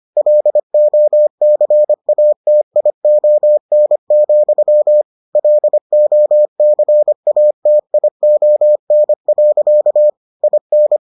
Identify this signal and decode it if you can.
Morse code, heard as a keyed tone, about 25 wpm, 600 Hz. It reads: LOCATION, LOCATION. IN